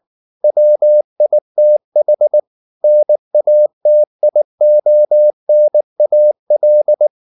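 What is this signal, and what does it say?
Morse code, heard as a keyed tone, about 19 wpm, 600 Hz. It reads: WITH NATIONAL
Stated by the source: Morse code at 19 wpm from CW med Gnister, reading Ken Follett's novel Never in English.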